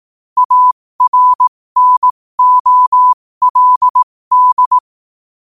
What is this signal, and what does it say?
Morse code, heard as a keyed tone, about 18 wpm, spaced as for 15 wpm, 1000 Hz. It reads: ARNOLD
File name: qotdfiles / 15wpm